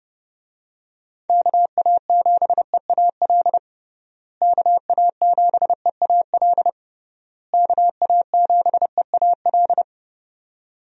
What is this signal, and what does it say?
Morse code, heard as a keyed tone, about 30 wpm, 700 Hz. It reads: KA7EAL KA7EAL KA7EAL
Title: USA Call Signs - 6 Characters - Repeated 3x - 30wpm